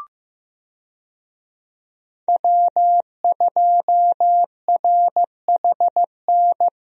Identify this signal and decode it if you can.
Morse code, heard as a keyed tone, about 15 wpm, 700 Hz. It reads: W2RHN
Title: USA Call Signs - 5 Characters - Repeated 2x - 15wpm